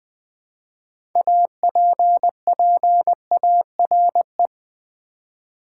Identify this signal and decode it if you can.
Morse code, heard as a keyed tone, about 20 wpm, 700 Hz. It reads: APPARE
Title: Build A Word - 2 Letters At A Time - 8-Letter Words - 20wpm